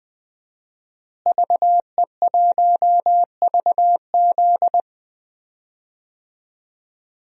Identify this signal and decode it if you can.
Morse code, heard as a keyed tone, about 20 wpm, 700 Hz. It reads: VE1VZ
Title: DXCC-100 Call Signs - 5 Characters - Repeated 3x - 20wpm